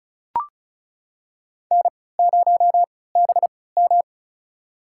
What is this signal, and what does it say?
Morse code, heard as a keyed tone, about 36 wpm, spaced as for 20 wpm, 700 Hz. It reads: N0BM
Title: USA Call Signs - 4 Characters - 3x Character Spacing - 35wpm